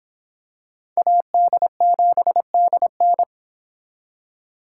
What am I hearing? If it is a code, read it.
Morse code, heard as a keyed tone, about 25 wpm, 700 Hz. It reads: AD7DN